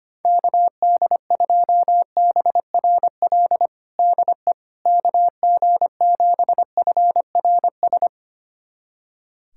Morse code, heard as a keyed tone, about 25 wpm, 700 Hz. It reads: KD2BRL DE KG7FRS
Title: USA Callsign DE Callsign 25wpm